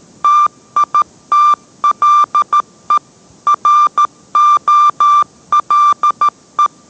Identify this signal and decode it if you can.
Morse code, heard as a keyed tone, about 13 wpm, 1180 Hz. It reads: TITLE ROLE